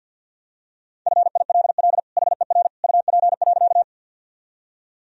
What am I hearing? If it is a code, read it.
Morse code, heard as a keyed tone, about 50 wpm, 700 Hz. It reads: FILL HER UP.
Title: Sentences from Top 1000 Words - Review Part 3 - 50wpm